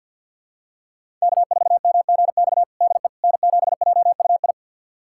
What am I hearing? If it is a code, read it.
Morse code, heard as a keyed tone, about 49 wpm, 700 Hz. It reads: K4MGX DE N7JUI
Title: USA Callsign DE Callsign - 5 Characters - 50wpm